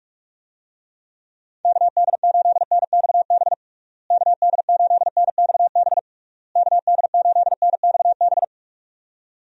Morse code, heard as a keyed tone, about 42 wpm, 700 Hz. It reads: KD8NXB KD8NXB KD8NXB
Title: USA Call Signs - 6 Characters - Repeated 3x - 45wpm